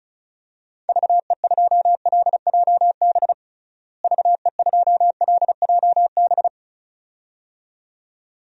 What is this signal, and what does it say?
Morse code, heard as a keyed tone, about 36 wpm, 700 Hz. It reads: VE2LJB VE2LJB
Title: DXCC-100 Call Signs - 6 Characters - Repeated 2x - 35wpm